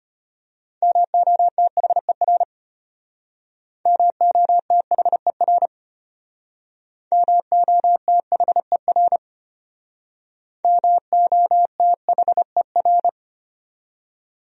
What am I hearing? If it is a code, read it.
Morse code, heard as a keyed tone, about 32 wpm, 700 Hz. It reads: MOTHER MOTHER MOTHER MOTHER